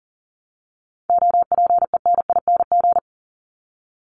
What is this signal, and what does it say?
Morse code, heard as a keyed tone, about 40 wpm, 700 Hz. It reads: OPENING
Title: ING Suffix - Top 100 Words with Base Words 40wpm